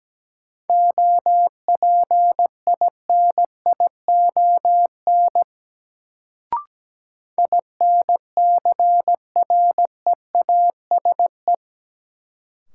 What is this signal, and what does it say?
Morse code, heard as a keyed tone, about 17 wpm, 700 Hz. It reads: OPINION INCREASE
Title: Sentences from Top 1200 Words - New Word Drill Alt - 17wpm